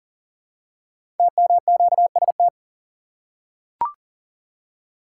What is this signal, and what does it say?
Morse code, heard as a keyed tone, about 38 wpm, 700 Hz. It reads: TMQST
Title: Code Groups - 5 Letter:Numbers - 40wpm